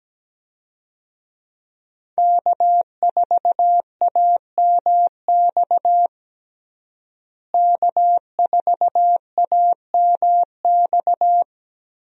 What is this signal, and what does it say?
Morse code, heard as a keyed tone, about 17 wpm, 700 Hz. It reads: K4AMX K4AMX
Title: USA Call Signs - 5 Characters - Repeated 2x - 17wpm